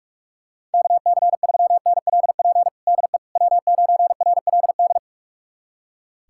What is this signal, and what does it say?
Morse code, heard as a keyed tone, about 46 wpm, 690 Hz. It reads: KC3NLP DE W9RLD
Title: USA Callsign DE Callsign 45wpm